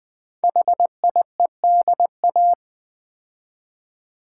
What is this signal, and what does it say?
Morse code, heard as a keyed tone, about 20 wpm, 700 Hz. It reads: HIEDA
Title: Code Groups - 5 Letters - 20wpm